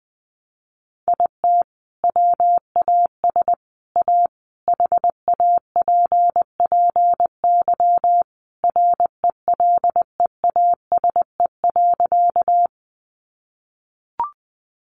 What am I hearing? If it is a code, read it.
Morse code, heard as a keyed tone, about 20 wpm, 700 Hz. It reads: IT WAS A HAPPY RELEASE.
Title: Sentences from Top 1100 Words 20wpm